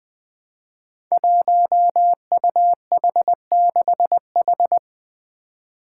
Morse code, heard as a keyed tone, about 20 wpm, 700 Hz. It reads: 1UH6H
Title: Five Challenging Letter-Number Combinations 20wpm